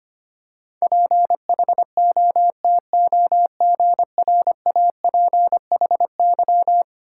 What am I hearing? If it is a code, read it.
Morse code, heard as a keyed tone, about 25 wpm, 700 Hz. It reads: PHOTOGRAPHY
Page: morse-code-ninja-practice-25wpm